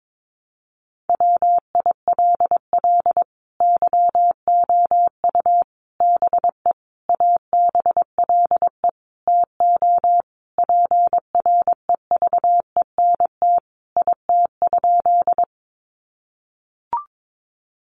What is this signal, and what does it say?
Morse code, heard as a keyed tone, about 22 wpm, 700 Hz. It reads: WILL YOU BE ABLE TO PREVENT IT?
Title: Sentences from Top 1100 Words 22wpm